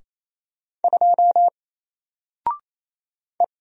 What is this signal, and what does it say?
Morse code, heard as a keyed tone, about 28 wpm, 700 Hz. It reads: 2 E
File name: Single-Letter-Number-28wpm